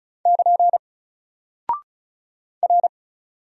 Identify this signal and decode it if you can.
Morse code, heard as a keyed tone, about 35 wpm, 700 Hz. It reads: <KN> R
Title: Single Character (Letter:Number:Punctuation:Prosign) 35wpm